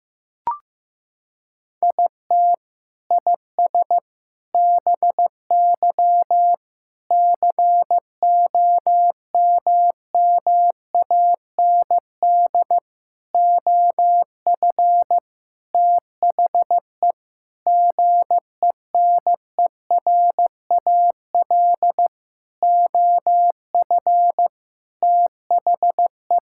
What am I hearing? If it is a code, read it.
Morse code, heard as a keyed tone, about 15 wpm, 700 Hz. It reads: IT IS BY COMMAND OF THE GENERAL OF THE
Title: Sentences from Top 1600 Words 15wpm